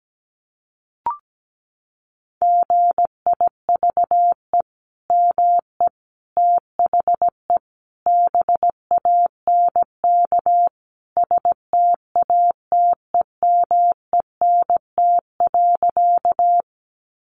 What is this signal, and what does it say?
Morse code, heard as a keyed tone, about 17 wpm, 700 Hz. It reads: GIVE ME THE BANK STATEMENT.
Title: Sentences from Top 1200 Words 17wpm